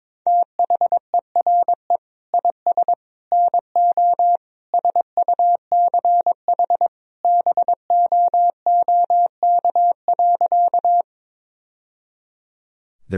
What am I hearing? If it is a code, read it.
Morse code, heard as a keyed tone, about 22 wpm, 700 Hz. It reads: THERE IS NO SUCH BOOK.
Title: Sentences from Top 1000 Words - Review Part 1 - 22wpm